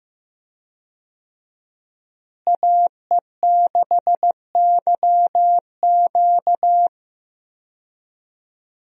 Morse code, heard as a keyed tone, about 15 wpm, 700 Hz. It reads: AE6YQ
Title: USA Call Signs - 5 Characters - Repeated 3x - 15wpm